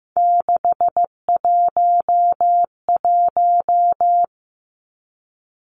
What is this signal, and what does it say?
Morse code, heard as a keyed tone, about 15 wpm, 700 Hz. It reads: 611
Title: Three Challenging Letter-Number Combinations 15wpm